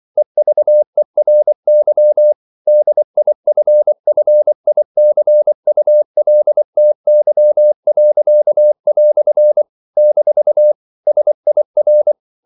Morse code, heard as a keyed tone, about 24 wpm, 600 Hz. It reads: EVERY DIFFICULTY." - SIR
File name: CW / Practice / EnglishQuotations0424